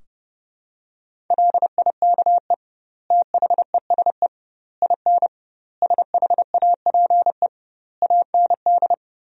Morse code, heard as a keyed tone, about 28 wpm, 700 Hz. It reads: LIKE THESE IN SHAPE AND
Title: 2x Word-Spacing - Sentences from Top 500 Words - 30wpm